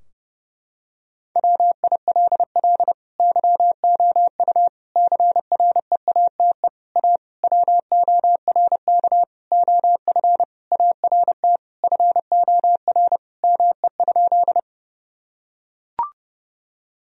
Morse code, heard as a keyed tone, about 30 wpm, 700 Hz. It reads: WILL YOU CREATE A WORK OF ART FOR ME?
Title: Sentences from Top 1000 Words - Review Part 3 - 30wpm